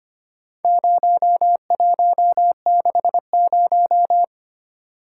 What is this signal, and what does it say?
Morse code, heard as a keyed tone, about 25 wpm, 700 Hz. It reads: 0160
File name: practice / 25wpm / Four Letter-Number Combinations - 25wpm